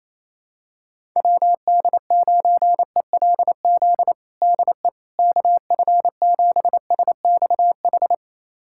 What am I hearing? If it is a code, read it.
Morse code, heard as a keyed tone, about 28 wpm, 700 Hz. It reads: WD9ELZ DE KF7SXH
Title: USA Callsign DE Callsign - 6 Characters - 28wpm